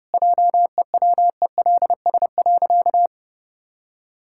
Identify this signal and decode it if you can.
Morse code, heard as a keyed tone, about 30 wpm, 700 Hz. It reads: JEWELS.
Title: Pangram Challenge v2 - 30wpm